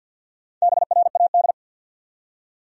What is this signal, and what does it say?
Morse code, heard as a keyed tone, about 48 wpm, 700 Hz. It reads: BRAD
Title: USA Common Names - Max 4 Letters - 50wpm